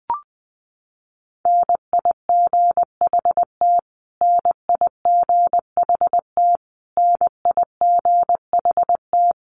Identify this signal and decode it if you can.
Morse code, heard as a keyed tone, about 20 wpm, 700 Hz. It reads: NIGHT NIGHT NIGHT
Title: Top 500 Words Repeated 20wpm